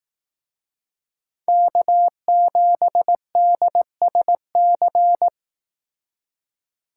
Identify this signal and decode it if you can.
Morse code, heard as a keyed tone, about 18 wpm, 700 Hz. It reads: K7DSC
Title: Speed Racing - USA Call Signs 15wpm